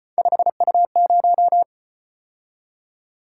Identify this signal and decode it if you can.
Morse code, heard as a keyed tone, about 33 wpm, 700 Hz. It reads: 5U0